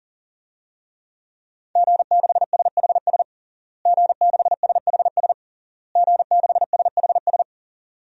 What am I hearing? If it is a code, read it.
Morse code, heard as a keyed tone, about 40 wpm, 700 Hz. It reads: G6SHS G6SHS G6SHS
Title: DXCC-100 Call Signs - 5 Characters - Repeated 3x - 40wpm